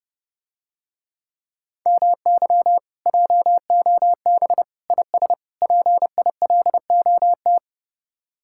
Morse code, heard as a keyed tone, about 30 wpm, 700 Hz. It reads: MY JOB IS PILOT